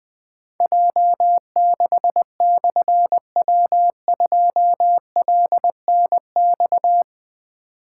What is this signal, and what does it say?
Morse code, heard as a keyed tone, about 20 wpm, 700 Hz. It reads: J6/W2LNX
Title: Difficult Call Signs 20wpm